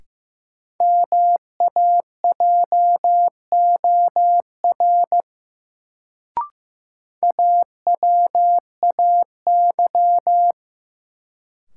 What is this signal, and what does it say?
Morse code, heard as a keyed tone, about 15 wpm, 700 Hz. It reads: MAJOR AWAY
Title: Sentences from Top 600 Words - New Word Drill - 15wpm